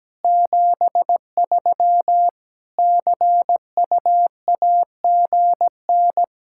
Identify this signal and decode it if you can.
Morse code, heard as a keyed tone, about 17 wpm, 700 Hz. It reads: 73 CUAGN
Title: K1USN SST QSO Practice 17wpm